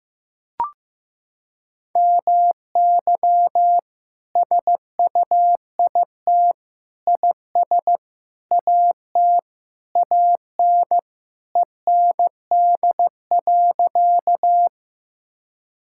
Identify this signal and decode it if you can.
Morse code, heard as a keyed tone, about 15 wpm, 700 Hz. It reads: MY SUIT IS AT AN END.
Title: Sentences from Top 1600 Words 15wpm